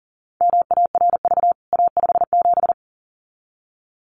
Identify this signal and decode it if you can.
Morse code, heard as a keyed tone, about 40 wpm, 700 Hz. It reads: MARV A57
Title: CWT Exchange 40wpm